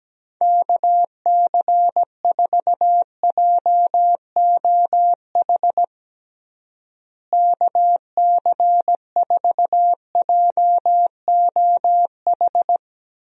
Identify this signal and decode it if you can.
Morse code, heard as a keyed tone, about 17 wpm, 700 Hz. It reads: KC4JOH KC4JOH
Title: USA Call Signs - 6 Characters - Repeated 2x - 17wpm